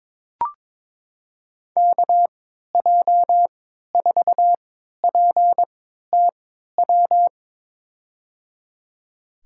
Morse code, heard as a keyed tone, about 22 wpm, spaced as for 12 wpm, 700 Hz. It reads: KJ4PTW
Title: USA Call Signs - 6 Characters - 3x Character Spacing - 22wpm